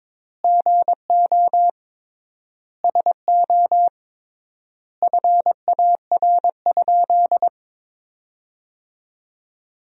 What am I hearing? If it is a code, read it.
Morse code, heard as a keyed tone, about 22 wpm, 700 Hz. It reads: GO SO FAR?
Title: 3x Word-Spacing - Sentences from Top 300 Words - 22wpm